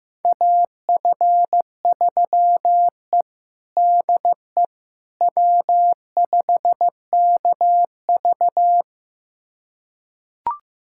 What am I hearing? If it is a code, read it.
Morse code, heard as a keyed tone, about 16 wpm, 700 Hz. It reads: AF3E DE W5KV
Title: USA Callsign DE Callsign - 4 Characters - 15wpm